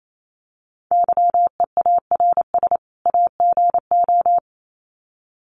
Morse code, heard as a keyed tone, about 28 wpm, 700 Hz. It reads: YEARS AGO
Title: Sets of 2 Words 28wpm